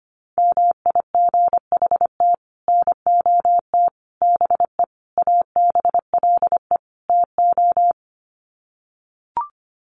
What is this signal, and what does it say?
Morse code, heard as a keyed tone, about 25 wpm, 700 Hz. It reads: MIGHT NOT BE ABLE TO
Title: Sets of 5 Words Encore2 25wpm